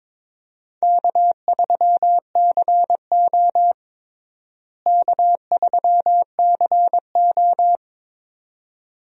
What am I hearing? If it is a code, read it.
Morse code, heard as a keyed tone, about 22 wpm, 700 Hz. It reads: K3CO K3CO
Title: USA Call Signs - 4 Characters - Repeated 2x - 22wpm